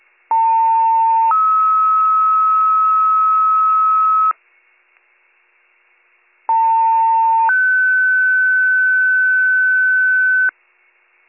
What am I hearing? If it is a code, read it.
Two-tone page, heard as a one-second tone, then a three-second tone, about 900 then 1290 Hz, about 900 then 1500 Hz, with noise. It is heard from 0.3 to 10.5 s.